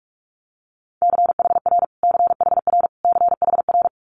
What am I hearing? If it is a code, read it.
Morse code, heard as a keyed tone, about 46 wpm, 700 Hz. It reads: CHR CHR CHR
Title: Multi-letter Phonemes Repeated 3x 45wpm